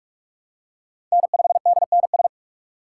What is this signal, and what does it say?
Morse code, heard as a keyed tone, about 46 wpm, 700 Hz. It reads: N5DNS
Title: USA Call Signs - 5 Characters - 2x Character Spacing - 45wpm